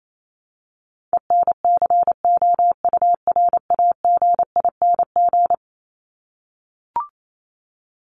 Morse code, heard as a keyed tone, about 28 wpm, 700 Hz. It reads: ENCOURAGING
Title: Top 1751-2000 Words 28wpm